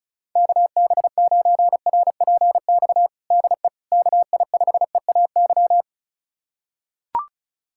Morse code, heard as a keyed tone, about 36 wpm, 700 Hz. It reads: KB9RPX DE KI5EAY
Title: USA Callsign DE Callsign - 6 Characters - 35wpm